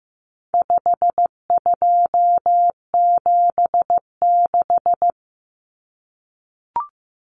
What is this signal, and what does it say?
Morse code, heard as a keyed tone, about 15 wpm, 700 Hz. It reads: 5276
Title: Numbers - 4-Digit 15wpm